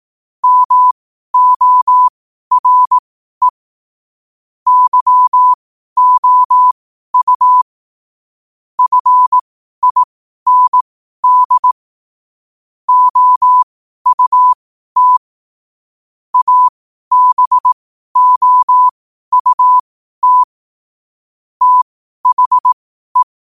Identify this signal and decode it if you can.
Morse code, heard as a keyed tone, about 18 wpm, spaced as for 12 wpm, 1000 Hz. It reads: MORE YOU FIND OUT ABOUT THE
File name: qotdfiles / 12wpm